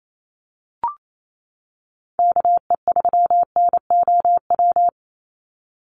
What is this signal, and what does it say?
Morse code, heard as a keyed tone, about 28 wpm, 700 Hz. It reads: KE3NOW
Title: USA Call Signs 28wpm